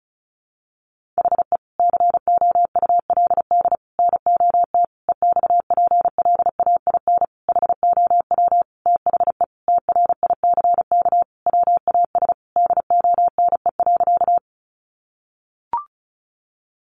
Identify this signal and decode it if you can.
Morse code, heard as a keyed tone, about 34 wpm, 700 Hz. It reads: HE COULD NOT EXPLAIN HOW THE TRICK WAS DONE.
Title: Sentences from Top 1500 Words 35wpm